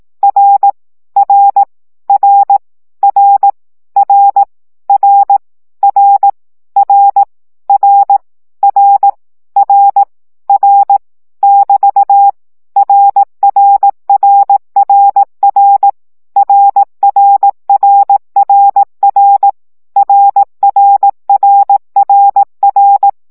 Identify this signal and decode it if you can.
Morse code, heard as a keyed tone, about 18 wpm, 800 Hz. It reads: R R R R R R R R R R R R <BT> RRRRR RRRRR RRRRR
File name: CW gyakorlas r 15 baud